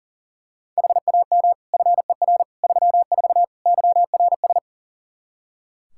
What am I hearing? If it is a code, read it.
Morse code, heard as a keyed tone, about 40 wpm, 700 Hz. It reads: HAM FER 34 YRS
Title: Practice QSO - Exchange 3 - More Info - 40wpm